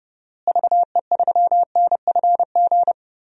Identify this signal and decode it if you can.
Morse code, heard as a keyed tone, about 30 wpm, 700 Hz. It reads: VE3NFG
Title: DXCC-100 Call Signs - 6 Characters - Repeated 2x - 30wpm